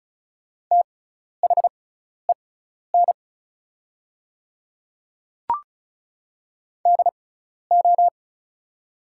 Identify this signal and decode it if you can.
Morse code, heard as a keyed tone, about 36 wpm, spaced as for 12 wpm, 690 Hz. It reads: THEN DO
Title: 6x Character Spacing - Top 500 Words - 35wpm